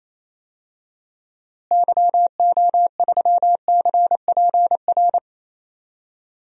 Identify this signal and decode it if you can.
Morse code, heard as a keyed tone, about 28 wpm, 700 Hz. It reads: YO3CPR
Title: DXCC-100 Call Signs - 6 Characters - Repeated 2x - 28wpm